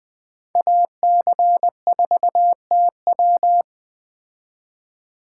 Morse code, heard as a keyed tone, about 20 wpm, 700 Hz. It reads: AC4TW
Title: Speed Racing - USA 2x2 Call Signs 17wpm